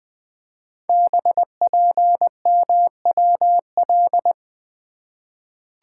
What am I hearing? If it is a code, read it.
Morse code, heard as a keyed tone, about 20 wpm, 700 Hz. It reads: BPMWL
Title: Code Groups - 5 Letters - 20wpm